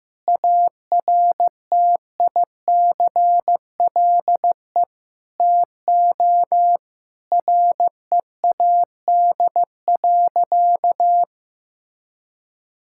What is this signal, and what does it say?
Morse code, heard as a keyed tone, about 15 wpm, 700 Hz. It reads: ARTICLE TO READ.
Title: Sentences from Top 1000 Words - Review Part 5 - 15wpm